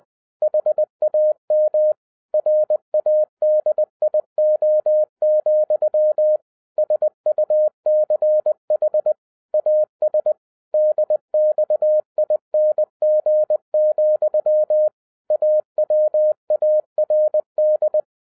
Morse code, heard as a keyed tone, about 20 wpm, 610 Hz. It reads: HAM RADIO, SUCH AS DXING, AWARD
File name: 2011-10-mp3 / n13-TIPS FROM MY 34 YEARS OF EXPERIENCE 20wpm